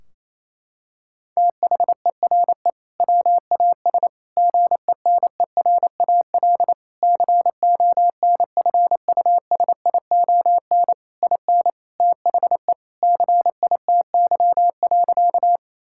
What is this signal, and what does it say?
Morse code, heard as a keyed tone, about 28 wpm, 700 Hz. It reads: THERE WAS GENERAL CONFUSION IN THE CITY.
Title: Sentences from Top 1600 Words 28wpm